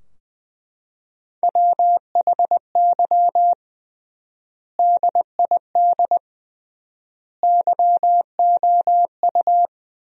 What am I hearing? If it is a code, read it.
Morse code, heard as a keyed tone, about 20 wpm, 700 Hz. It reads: WHY DID YOU
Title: 3x Word-Spacing - Sentences from Top 500 Words - 20wpm